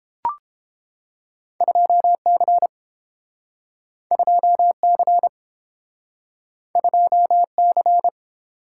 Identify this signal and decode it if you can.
Morse code, heard as a keyed tone, about 30 wpm, 700 Hz. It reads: 2C 2C 2C